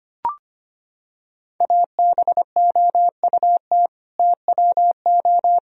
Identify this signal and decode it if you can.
Morse code, heard as a keyed tone, about 25 wpm, 700 Hz. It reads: ABOUT TWO
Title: Sentences from Top 1000 Words - Review Part 2 - 25wpm